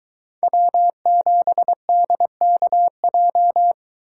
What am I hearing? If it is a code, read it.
Morse code, heard as a keyed tone, about 21 wpm, 700 Hz. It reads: W7DKJ